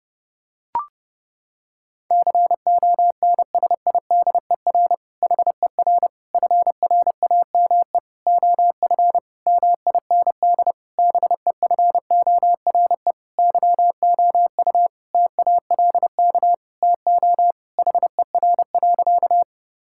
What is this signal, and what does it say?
Morse code, heard as a keyed tone, about 32 wpm, 700 Hz. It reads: CONSIDER HER FRAME OF MIND BEFORE YOU TALK TO HER.
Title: Sentences from Top 1000 Words - Review Part 6 - 30wpm